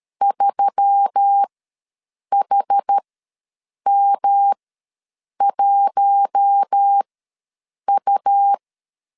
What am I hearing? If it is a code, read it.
Morse code, heard as a keyed tone, about 13 wpm, spaced as for 7 wpm, 800 Hz. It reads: 3HM1U
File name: LEZIONI / 5 LBVC / 5 L ws 12wpm 2958